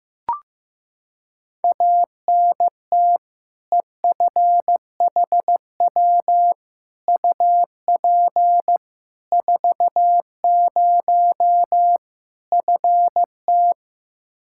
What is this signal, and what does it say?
Morse code, heard as a keyed tone, about 15 wpm, 700 Hz. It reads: ANT EFHW UP 40 FT
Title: K1USN SST QSO Practice 15wpm